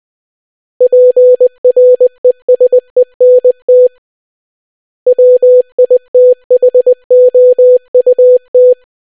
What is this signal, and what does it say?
Morse code, heard as a keyed tone, about 20 wpm, 500 Hz. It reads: PRESENT WITHOUT